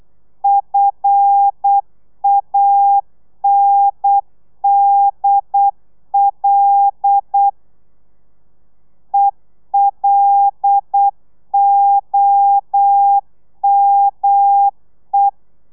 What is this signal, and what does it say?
Morse code, heard as a keyed tone, about 8 wpm, 800 Hz. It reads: FANDL ELOME
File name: telegrafia / i2tfj / Lez5 35